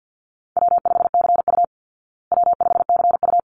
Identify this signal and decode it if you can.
Morse code, heard as a keyed tone, about 50 wpm, 700 Hz. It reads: W5CU W5CU
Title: Easy Call Signs Repeated 2x - 50wpm